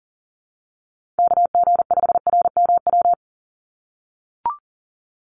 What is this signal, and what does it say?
Morse code, heard as a keyed tone, about 40 wpm, 700 Hz. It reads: KG5RMW
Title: USA Call Signs 40wpm